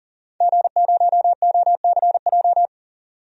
Morse code, heard as a keyed tone, about 37 wpm, 700 Hz. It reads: G0OCJ
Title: DXCC-100 Call Signs - 5 Characters - 40wpm